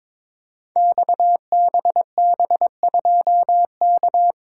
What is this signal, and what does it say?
Morse code, heard as a keyed tone, about 22 wpm, 700 Hz. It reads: XBB2K
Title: Code Groups - 5 Letter:Numbers - 22wpm